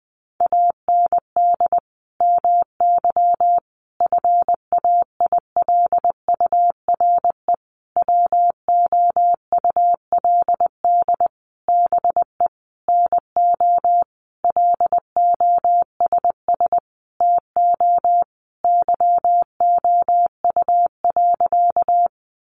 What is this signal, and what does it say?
Morse code, heard as a keyed tone, about 20 wpm, 700 Hz. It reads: AND MY FAILURE WOULD BE NO LOSS TO YOU.
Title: Sentences from Top 1400 Words 20wpm